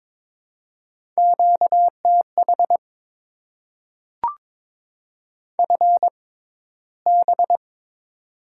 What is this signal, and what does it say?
Morse code, heard as a keyed tone, about 22 wpm, spaced as for 14 wpm, 700 Hz. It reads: QTH FB
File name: Top 20 QSO Elements - 6x Character Spacing - 22wpm